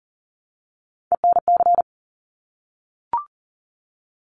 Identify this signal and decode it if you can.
Morse code, heard as a keyed tone, about 40 wpm, 700 Hz. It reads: ENC
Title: Top Three Letter Combinations in Words 40wpm